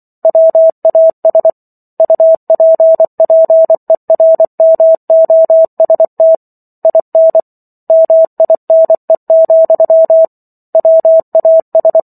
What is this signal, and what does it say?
Morse code, heard as a keyed tone, about 24 wpm, 650 Hz. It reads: WAS UPPERMOST IN MINE, WAS